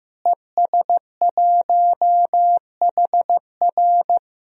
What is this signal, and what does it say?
Morse code, heard as a keyed tone, about 16 wpm, 700 Hz. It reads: ES1HR
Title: DXCC-100 Call Signs - 5 Characters - Repeated 3x - 15wpm